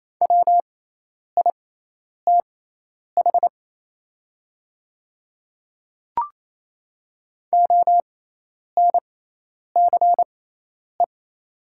Morse code, heard as a keyed tone, about 28 wpm, spaced as for 10 wpm, 700 Hz. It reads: WITH ONCE
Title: 6x Character Spacing - Top 300 Words - 28wpm